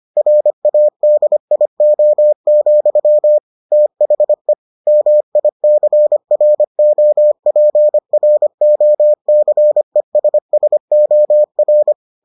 Morse code, heard as a keyed tone, about 25 wpm, 600 Hz. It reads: RADIO, THE MICROPROCESSOR